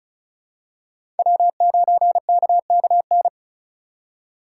Morse code, heard as a keyed tone, about 36 wpm, 700 Hz. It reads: W9KKN
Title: USA Call Signs - 5 Characters - 4x Character Spacing - 35wpm